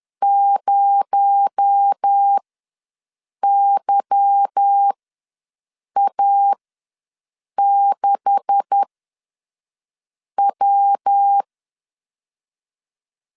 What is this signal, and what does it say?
Morse code, heard as a keyed tone, about 10 wpm, spaced as for 5 wpm, 800 Hz. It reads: 0YA6W
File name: LEZIONI / 9 286 ws 10wpm 4032